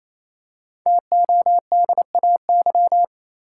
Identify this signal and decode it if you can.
Morse code, heard as a keyed tone, about 28 wpm, 700 Hz. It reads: TODAY
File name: Easy Words - Speed-Racing with Hints - 22wpm